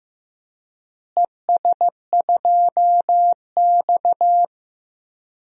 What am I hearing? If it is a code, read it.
Morse code, heard as a keyed tone, about 16 wpm, 700 Hz. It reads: ES2X
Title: DXCC-100 Call Signs - 4 Characters - Repeated 2x - 15wpm